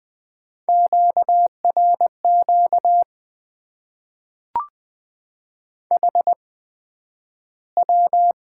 Morse code, heard as a keyed tone, about 20 wpm, spaced as for 11 wpm, 700 Hz. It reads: QRQ HW
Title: Top 90 QSO Elements - 8x Character Spacing - 20wpm